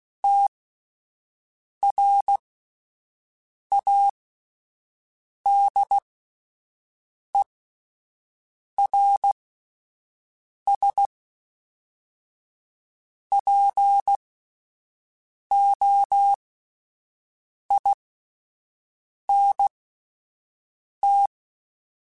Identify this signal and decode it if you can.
Morse code, heard as a keyed tone, about 16 wpm, spaced as for 5 wpm, 800 Hz. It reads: TRADERS POINT